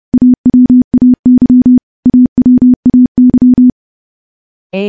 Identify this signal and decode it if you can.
Morse code, heard as a keyed tone, about 30 wpm, 250 Hz. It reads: AWAY AWAY
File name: morse / morse-training-30WPM